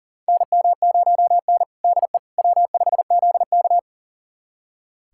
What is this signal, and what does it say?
Morse code, heard as a keyed tone, about 40 wpm, 700 Hz. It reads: NM0N DE W5ZK
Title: USA Callsign DE Callsign - 4 Characters - 40wpm